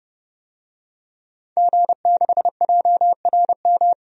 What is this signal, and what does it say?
Morse code, heard as a keyed tone, about 29 wpm, 700 Hz. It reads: G6JRM
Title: DXCC-100 Call Signs - 5 Characters - 30wpm